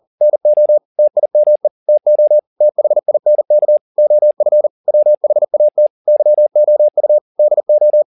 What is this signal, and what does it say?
Morse code, heard as a keyed tone, about 40 wpm, 600 Hz. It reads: NO TIME TO THINK OF WHAT YOU DO
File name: the old man and the sea 2 40wpm